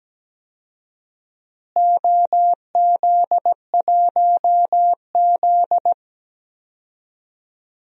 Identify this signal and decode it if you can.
Morse code, heard as a keyed tone, about 17 wpm, 700 Hz. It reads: OZ1Z
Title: DXCC-100 Call Signs - 4 Characters - Repeated 2x - 17wpm